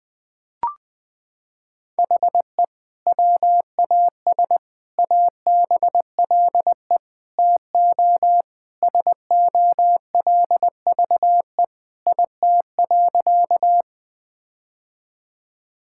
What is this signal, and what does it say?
Morse code, heard as a keyed tone, about 20 wpm, 700 Hz. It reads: HE WAS ABLE TO SOLVE IT.
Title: Sentences with 3-Word Phrases - 20wpm